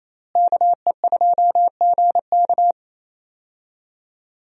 Morse code, heard as a keyed tone, about 28 wpm, 700 Hz. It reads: KE2GK
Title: USA Call Signs - 5 Characters - Repeated 3x - 28wpm